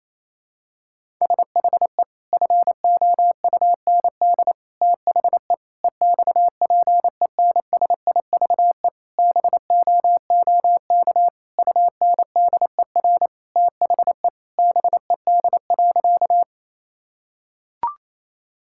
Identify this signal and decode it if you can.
Morse code, heard as a keyed tone, about 28 wpm, 700 Hz. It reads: SHE FOUND THE EXPENSIVE BOOK UNDER THE BED.
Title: Sentences from Top 1000 Words - Review Part 6 - 28wpm